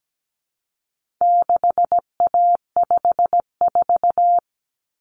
Morse code, heard as a keyed tone, about 17 wpm, 700 Hz. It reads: 6A54
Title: Common Cut Numbers - 4-Digit 17wpm